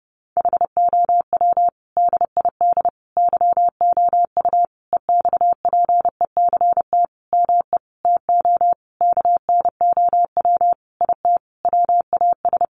Morse code, heard as a keyed tone, about 30 wpm, 700 Hz. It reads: HOW DID YOU EXPECT ME TO KNOW IT WAS
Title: Sentences from Top 1100 Words 30wpm